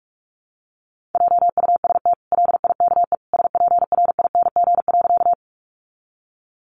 Morse code, heard as a keyed tone, about 44 wpm, 700 Hz. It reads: JUST LIKE SPRING.
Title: Sentences from Top 1200 Words 45wpm